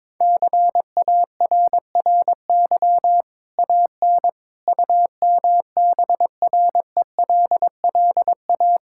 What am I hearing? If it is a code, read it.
Morse code, heard as a keyed tone, about 22 wpm, 700 Hz. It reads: CARRY AN UMBRELLA
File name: practice / 22wpm / Easy Sentences with Hints - 22wpm